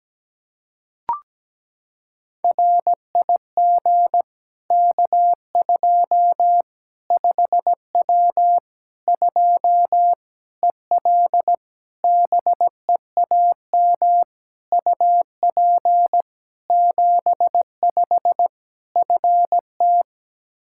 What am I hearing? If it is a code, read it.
Morse code, heard as a keyed tone, about 17 wpm, 700 Hz. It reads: RIG K2 5W 2 EL BEAM UP 75 FT
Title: Practice QSO - Exchange 2 - Info - 17wpm